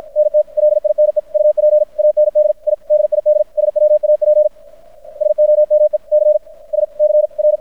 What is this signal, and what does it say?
Morse code, heard as a keyed tone, about 53 wpm, 600 Hz. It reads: NT LINE UP AND I DID SWAP SOME C S C R
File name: archived / CRSnet